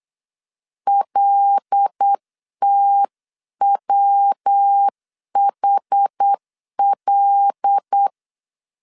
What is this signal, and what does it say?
Morse code, heard as a keyed tone, about 8 wpm, 800 Hz. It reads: LTWHL
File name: CW / DET / 1 6 1500C S C 8wpm 5219